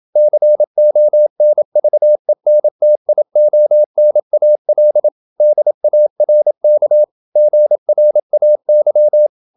Morse code, heard as a keyed tone, about 27 wpm, 600 Hz. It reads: CONVENTIONAL DARK GRAY
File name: Never 0027wpm